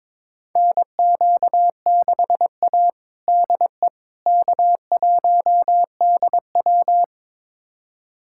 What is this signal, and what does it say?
Morse code, heard as a keyed tone, about 22 wpm, 700 Hz. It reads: NQ6A DE K1DW
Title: USA Callsign DE Callsign - 4 Characters - 22wpm